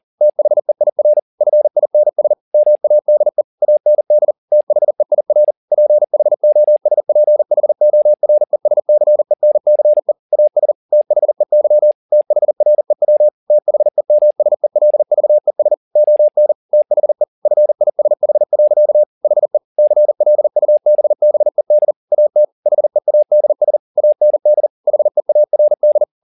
Morse code, heard as a keyed tone, about 40 wpm, 600 Hz. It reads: THEIR FINS MADE AND THEIR PHOSPHORESCENCE AS THEY THREW THEMSELVES ON THE FISH. HE CLUBBED AT HEADS AND HEARD
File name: the old man and the sea 2 40wpm